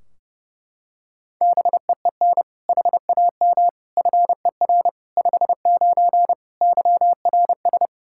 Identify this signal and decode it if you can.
Morse code, heard as a keyed tone, about 30 wpm, 700 Hz. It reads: BEEN HAM FER 59 YRS
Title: Practice QSO - Exchange 3 - More Info - 30wpm